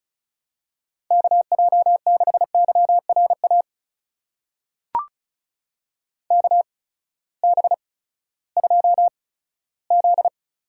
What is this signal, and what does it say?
Morse code, heard as a keyed tone, about 36 wpm, spaced as for 18 wpm, 700 Hz. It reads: KJ6YRA KB2Z
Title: USA Call Signs - 6 Characters - 8x Character Spacing - 35wpm